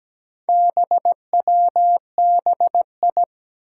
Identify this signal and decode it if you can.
Morse code, heard as a keyed tone, about 17 wpm, 700 Hz. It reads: BWBI
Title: Four Challenging Letter-Number Combinations 17wpm